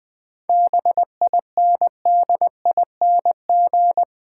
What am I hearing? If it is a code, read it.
Morse code, heard as a keyed tone, about 20 wpm, 700 Hz. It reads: BINDING
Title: Difficult Words - Speed-Racing with Hints - 20wpm